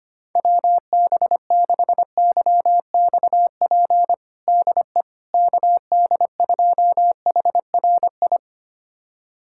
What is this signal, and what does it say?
Morse code, heard as a keyed tone, about 25 wpm, 700 Hz. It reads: WB6YXP DE KD2HRI
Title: USA Callsign DE Callsign 25wpm